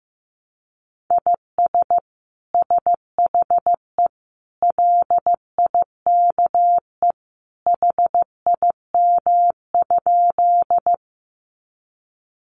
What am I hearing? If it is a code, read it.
Morse code, heard as a keyed tone, about 15 wpm, 700 Hz. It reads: IS SHE LIKE HIM?
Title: Sentences from Top 100 Words 15wpm